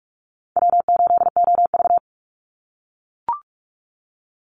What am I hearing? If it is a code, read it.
Morse code, heard as a keyed tone, about 44 wpm, 700 Hz. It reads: W8OV
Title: Easy Call Signs - 45wpm